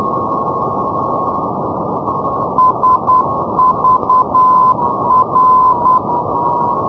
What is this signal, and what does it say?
Morse code, heard as a keyed tone, about 10 wpm, 1020 Hz, with radio noise. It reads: SVR
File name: europe / hng / hng-670 ndb sagvar svr-348+srb-78-tpl-348+hrv-240-sk-350-np-aut-poellau-2011-rb